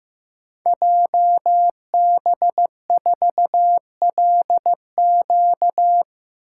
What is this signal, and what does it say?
Morse code, heard as a keyed tone, about 15 wpm, 700 Hz. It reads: JB4LQ
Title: Code Groups - 5 Letter:Numbers - 15wpm